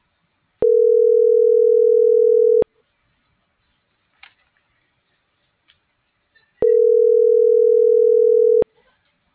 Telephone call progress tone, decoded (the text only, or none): ringback tone